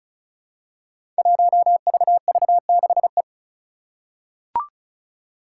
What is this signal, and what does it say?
Morse code, heard as a keyed tone, about 36 wpm, 700 Hz. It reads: 1VV6E
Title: Five Challenging Letter-Number Combinations 35wpm